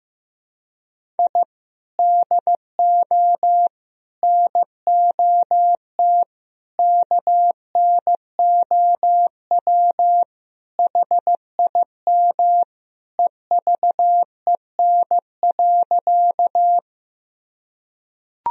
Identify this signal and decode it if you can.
Morse code, heard as a keyed tone, about 16 wpm, 700 Hz. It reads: I DO NOT KNOW HIM EVEN.
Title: Sentences from Top 1000 Words - Review Part 1 - 15wpm